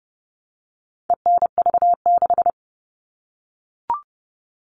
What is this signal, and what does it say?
Morse code, heard as a keyed tone, about 30 wpm, 700 Hz. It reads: ENV6
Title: All Cut Numbers - 4-Digit 30wpm